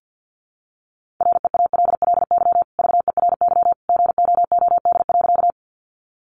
Morse code, heard as a keyed tone, about 50 wpm, 700 Hz. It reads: REALLY VERY GOOD.
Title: Sentences from Top 600 Words 50wpm